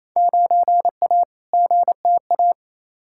Morse code, heard as a keyed tone, about 28 wpm, 700 Hz. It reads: 9A GTA
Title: Field Day Exchanges Encore - 28wpm